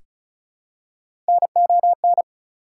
Morse code, heard as a keyed tone, about 34 wpm, 700 Hz. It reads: NON
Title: Prefixes and Suffixes 35wpm